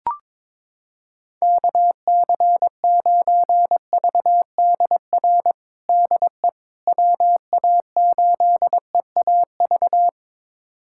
Morse code, heard as a keyed tone, about 22 wpm, 700 Hz. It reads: KC9VDR DE WA8EAV
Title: USA Callsign DE Callsign 22wpm